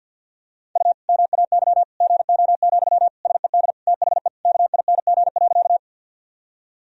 Morse code, heard as a keyed tone, about 52 wpm, 700 Hz. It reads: U MAY GO, SED THE KING.
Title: Sentences from Top 400 Words and Abbreviations 50wpm